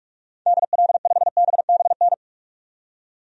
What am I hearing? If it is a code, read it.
Morse code, heard as a keyed tone, about 46 wpm, 700 Hz. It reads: DL5BBN
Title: DXCC-100 Call Signs - 6 Characters - 45wpm